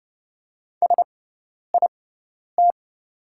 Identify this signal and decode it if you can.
Morse code, heard as a keyed tone, about 30 wpm, spaced as for 10 wpm, 690 Hz. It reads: SIT
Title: 6x Character Spacing - Top 500 Words - 30wpm